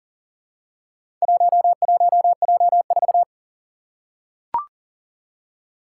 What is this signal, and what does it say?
Morse code, heard as a keyed tone, about 39 wpm, 700 Hz. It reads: 11J4
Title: Four Challenging Letter-Number Combinations 40wpm